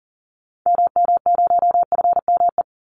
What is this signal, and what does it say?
Morse code, heard as a keyed tone, about 39 wpm, 700 Hz. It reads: MM0FME
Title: Call Signs - 40wpm